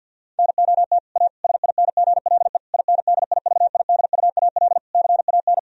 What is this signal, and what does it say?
Morse code, heard as a keyed tone, about 51 wpm, 700 Hz. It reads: NOT A SINGLE INDIVIDUAL CAN
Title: Sentences from Top 1000 Words - Review Part 4 - 50wpm